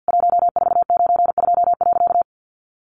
Morse code, heard as a keyed tone, about 50 wpm, 700 Hz. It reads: 1492.
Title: A Sound Of Thunder 50wpm